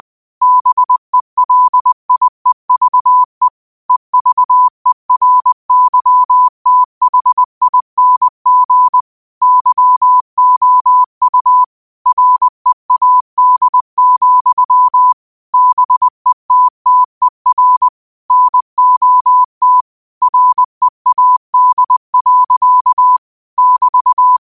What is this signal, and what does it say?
Morse code, heard as a keyed tone, about 20 wpm, 1000 Hz. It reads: BELIEVE EVERYTHING YOU READ, BETTER NOT READ. <BT>
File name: qotdfiles / 20wpm